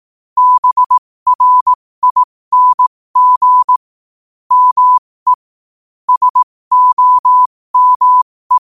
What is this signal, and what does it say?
Morse code, heard as a keyed tone, about 18 wpm, spaced as for 15 wpm, 1000 Hz. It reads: BRING ME SOME
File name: qotdfiles / 15wpm